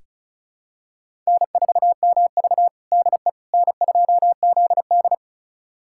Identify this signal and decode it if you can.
Morse code, heard as a keyed tone, about 36 wpm, 700 Hz. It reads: N4MV DE N2ZD
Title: USA Callsign DE Callsign - 4 Characters - 35wpm